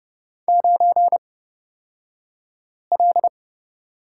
Morse code, heard as a keyed tone, about 30 wpm, spaced as for 5 wpm, 700 Hz. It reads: 9L
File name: Lesson 36d - BT - Two Characters 30wpm